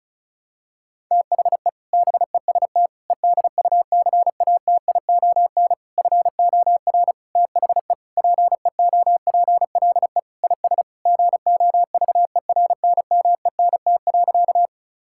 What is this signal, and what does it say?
Morse code, heard as a keyed tone, about 36 wpm, 700 Hz. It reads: THE BEST EDUCATION FOR THE PEOPLE IS GOVERNMENT.
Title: Sentences from Top 1000 Words - Review Part 6 - 35wpm